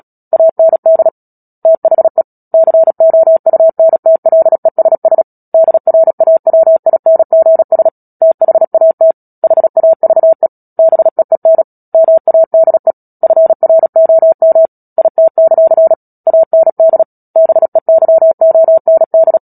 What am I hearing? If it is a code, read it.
Morse code, heard as a keyed tone, about 37 wpm, 650 Hz. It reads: AND THE COUNTLESS DRAWINGS THAT HAVE BEEN MADE FROM IT; AND BEYOND
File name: Book Two / 10-Chapter 10 - Epilogue - 36 WPM